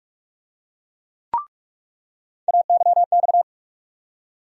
Morse code, heard as a keyed tone, about 47 wpm, 700 Hz. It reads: AYX
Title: USA Call Sign Postfixes - 3 Letters 45wpm